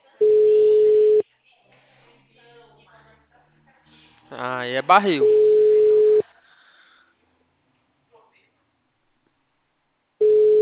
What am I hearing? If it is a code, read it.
Ringback tone (Europe, 425 Hz), with speech.